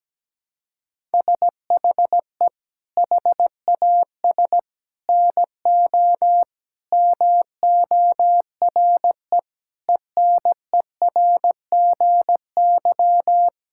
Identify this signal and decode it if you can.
Morse code, heard as a keyed tone, about 17 wpm, 700 Hz. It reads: SHE HAS NO MORE ENERGY
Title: Sentences from Top 1000 Words - Review Part 6 - 17wpm